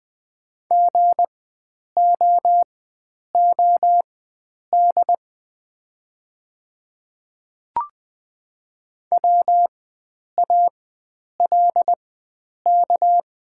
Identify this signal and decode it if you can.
Morse code, heard as a keyed tone, about 20 wpm, spaced as for 9 wpm, 700 Hz. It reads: GOOD WALK
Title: 4x Character Spacing - Top 300 Words - 20wpm